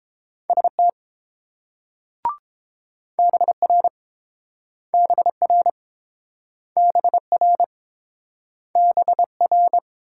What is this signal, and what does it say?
Morse code, heard as a keyed tone, about 30 wpm, 700 Hz. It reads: ST BR BR BR BR